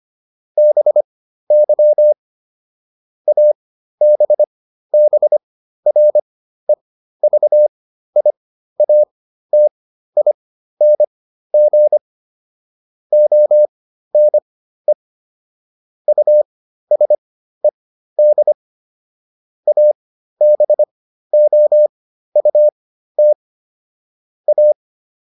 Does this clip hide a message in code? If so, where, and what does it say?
Yes: Morse code from 0.6 to 24.7 s, heard as a keyed tone, about 25 wpm, spaced as for 13 wpm, 600 Hz. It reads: BY ABBREVIATING ONE USED ABOUT A